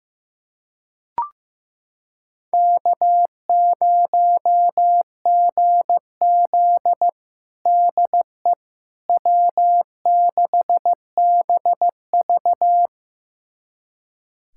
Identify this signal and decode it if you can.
Morse code, heard as a keyed tone, about 15 wpm, 700 Hz. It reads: K0GZ DE W6BV
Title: USA Callsign DE Callsign - 4 Characters - 15wpm